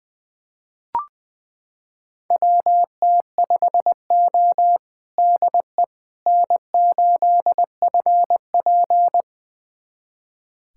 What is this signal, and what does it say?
Morse code, heard as a keyed tone, about 21 wpm, 700 Hz. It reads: WT5O DE N8FP
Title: USA Callsign DE Callsign - 4 Characters - 20wpm